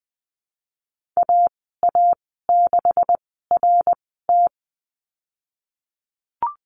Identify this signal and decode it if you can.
Morse code, heard as a keyed tone, about 20 wpm, spaced as for 15 wpm, 700 Hz. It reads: AA6RT
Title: 128 Lesson 21g - Number 6 - Call Signs 20wpm